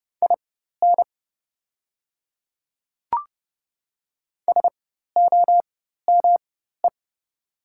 Morse code, heard as a keyed tone, about 30 wpm, spaced as for 14 wpm, 700 Hz. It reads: IN SOME
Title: 4x Character Spacing - Top 100 Words - 30wpm